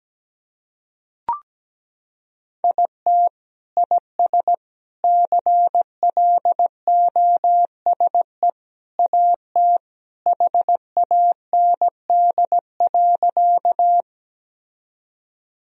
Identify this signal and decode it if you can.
Morse code, heard as a keyed tone, about 17 wpm, 700 Hz. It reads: IT IS CLOSE AT HAND.
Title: Sentences from Top 1000 Words - Review Part 1 - 17wpm